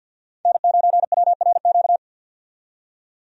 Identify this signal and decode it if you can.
Morse code, heard as a keyed tone, about 52 wpm, 700 Hz. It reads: N9WRQ
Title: USA Call Signs - 5 Characters - 50wpm